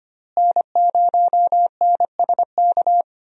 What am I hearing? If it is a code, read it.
Morse code, heard as a keyed tone, about 25 wpm, 700 Hz. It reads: N0NSK
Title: USA Call Signs - 5 Characters - Repeated 2x - 25wpm